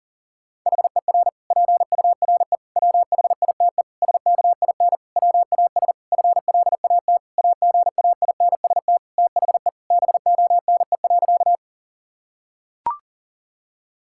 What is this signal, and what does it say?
Morse code, heard as a keyed tone, about 40 wpm, 700 Hz. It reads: HER PURE WHITE SKIN WAS FLAT AGAINST THE BONE.
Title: Sentences from Top 1600 Words 40wpm